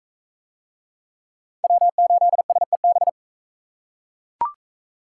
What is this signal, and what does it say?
Morse code, heard as a keyed tone, about 39 wpm, 700 Hz. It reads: W8SEB